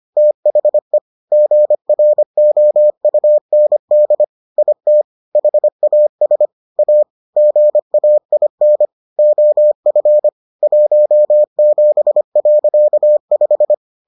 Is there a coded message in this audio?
Yes: Morse code from 0.2 to 13.7 s, heard as a keyed tone, about 25 wpm, 600 Hz. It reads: THE GROUND IT HAS A GAIN OF 17.5